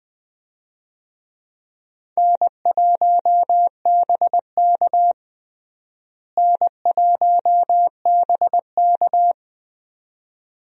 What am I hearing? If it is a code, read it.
Morse code, heard as a keyed tone, about 21 wpm, 700 Hz. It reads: N1BK N1BK
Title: USA Call Signs - 4 Characters - Repeated 2x - 20wpm